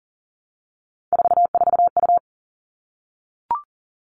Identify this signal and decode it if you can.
Morse code, heard as a keyed tone, about 40 wpm, 700 Hz. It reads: 44U
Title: All Cut Numbers - 3-Digit 40wpm